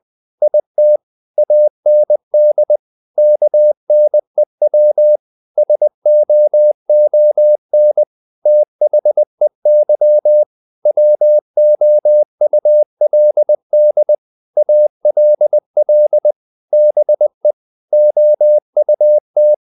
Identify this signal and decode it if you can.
Morse code, heard as a keyed tone, about 20 wpm, 600 Hz. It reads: IT AND KNEW SOON THEY WOULD ALL BE OUT
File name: the old man and the sea part 2 20wpm